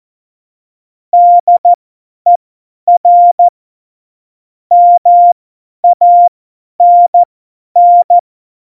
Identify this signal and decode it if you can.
Morse code, heard as a keyed tone, about 14 wpm, spaced as for 10 wpm, 700 Hz. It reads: DER MANN